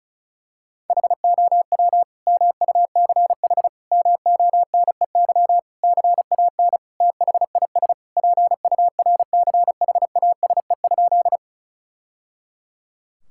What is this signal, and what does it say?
Morse code, heard as a keyed tone, about 34 wpm, 700 Hz. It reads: HOW MUCH MONEY CAN THIS PURCHASE?
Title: Sentences from Top 1000 Words - Review Part 5 - 35wpm